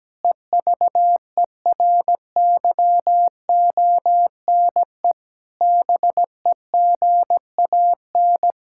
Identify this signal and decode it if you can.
Morse code, heard as a keyed tone, about 17 wpm, 700 Hz. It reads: EVERYONE BEGAN
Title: Sentences from Top 1000 Words - Review Part 5 - 17wpm